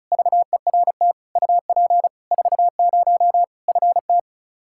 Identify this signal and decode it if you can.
Morse code, heard as a keyed tone, about 36 wpm, 700 Hz. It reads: VERT UP 40 FT
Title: Practice QSO - Exchange 2 - Info - 35wpm